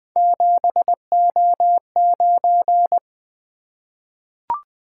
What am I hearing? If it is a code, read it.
Morse code, heard as a keyed tone, about 20 wpm, 700 Hz. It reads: 7O9
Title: Three Letter-Number Combinations - 20wpm